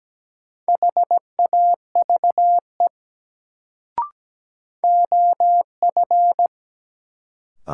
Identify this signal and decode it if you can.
Morse code, heard as a keyed tone, about 17 wpm, 700 Hz. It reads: HAVE OF
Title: Sets of 2 Words - New Word Drill - 17wpm